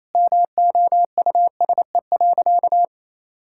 Morse code, heard as a keyed tone, about 28 wpm, 700 Hz. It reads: MOUSE.
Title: Sentences from Top 1000 Words - Review Part 5 - 28wpm